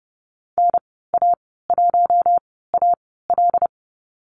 Morse code, heard as a keyed tone, about 30 wpm, spaced as for 17 wpm, 700 Hz. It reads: NA1AL
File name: Lesson 14g - Number 2 - Call Signs 30wpm